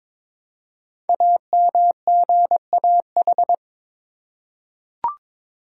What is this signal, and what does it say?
Morse code, heard as a keyed tone, about 22 wpm, 700 Hz. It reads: AMGAH
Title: Code Groups - 5 Letters - 22wpm